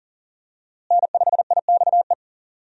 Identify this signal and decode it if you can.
Morse code, heard as a keyed tone, about 40 wpm, 700 Hz. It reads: N5IXE
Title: USA Call Signs - 5 Characters - Repeated 3x - 40wpm